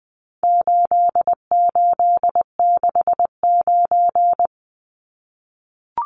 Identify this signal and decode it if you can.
Morse code, heard as a keyed tone, about 20 wpm, 700 Hz. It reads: 8869
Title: Numbers - 4-Digit 20wpm